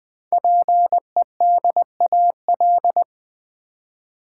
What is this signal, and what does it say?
Morse code, heard as a keyed tone, about 20 wpm, 700 Hz. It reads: PEDAL
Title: Easy Words with Hints - 20wpm